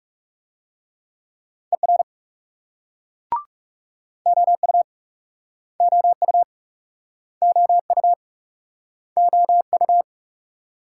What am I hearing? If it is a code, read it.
Morse code, heard as a keyed tone, about 43 wpm, 700 Hz. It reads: ER OU OU OU OU